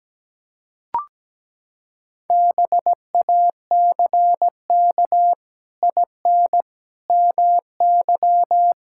Morse code, heard as a keyed tone, about 17 wpm, 700 Hz. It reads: BACK IN MY
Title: Pangram Challenge v2 - 17wpm